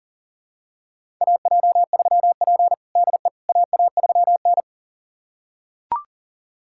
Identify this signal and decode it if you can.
Morse code, heard as a keyed tone, about 37 wpm, 700 Hz. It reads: AJ3P DE AA3N
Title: USA Callsign DE Callsign - 4 Characters - 40wpm